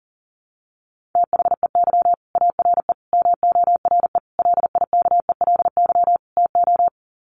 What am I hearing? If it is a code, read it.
Morse code, heard as a keyed tone, about 40 wpm, 700 Hz. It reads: THEY ARE MORE LIKELY TO
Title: Sets of 5 Words Encore2 40wpm